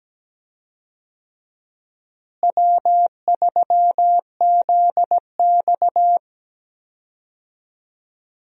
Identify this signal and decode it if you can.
Morse code, heard as a keyed tone, about 17 wpm, 700 Hz. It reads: W3ZX
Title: USA Call Signs - 4 Characters - Repeated 3x - 17wpm